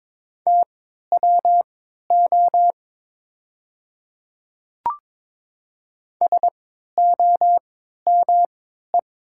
Morse code, heard as a keyed tone, about 22 wpm, spaced as for 12 wpm, 700 Hz. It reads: TWO SOME
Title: 3x Character Spacing - Top 100 Words - 22wpm